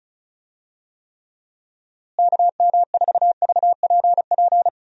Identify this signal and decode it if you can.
Morse code, heard as a keyed tone, about 34 wpm, 700 Hz. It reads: KM4VPP
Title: USA Call Signs - 6 Characters - Repeated 3x - 35wpm